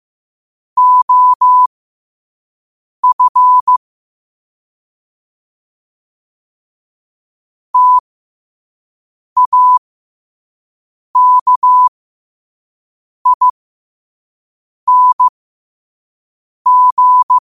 Morse code, heard as a keyed tone, about 15 wpm, spaced as for 5 wpm, 1000 Hz. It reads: OF TAKING
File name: qotdfiles / 5wpm